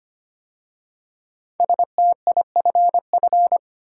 Morse code, heard as a keyed tone, about 25 wpm, 700 Hz. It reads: STIFF
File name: 5 Letter Words - First 3 Letters Given - 25wpm